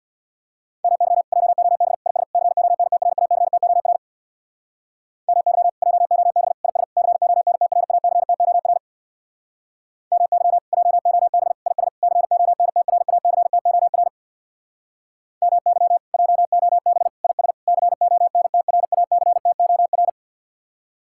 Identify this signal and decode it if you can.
Morse code, heard as a keyed tone, about 64 wpm, 700 Hz. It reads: MY JOB IS CONTRACTOR MY JOB IS CONTRACTOR MY JOB IS CONTRACTOR MY JOB IS CONTRACTOR